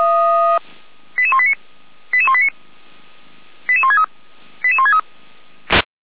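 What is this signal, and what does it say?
Five-tone selective call (ZVEI), heard as quick runs of five tones, tones of about 70 ms. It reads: 88189 88189 88163 88163
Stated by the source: ZVEI five-tone selective call received by SDR, from the Feuerwehr Orlamünde audio archive.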